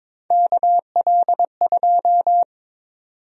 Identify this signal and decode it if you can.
Morse code, heard as a keyed tone, about 22 wpm, 700 Hz. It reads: KL2
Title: USA Call Sign Prefixes - 22wpm